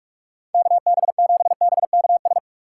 Morse code, heard as a keyed tone, about 46 wpm, 690 Hz. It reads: KB7BKS
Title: USA Call Signs - 3x Character Spacing - 45wpm